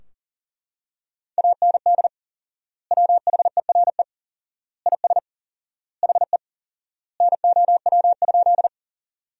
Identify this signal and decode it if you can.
Morse code, heard as a keyed tone, about 40 wpm, 700 Hz. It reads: AND WHERE IS HE NOW?
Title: 4x Word-Spacing - Sentences from Top 200 Words - 40wpm